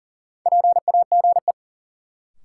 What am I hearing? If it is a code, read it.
Morse code, heard as a keyed tone, about 40 wpm, 700 Hz. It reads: PAGE
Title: Sentences from Top 1000 Words - Review Part 1 - New Word Drill - 40wpm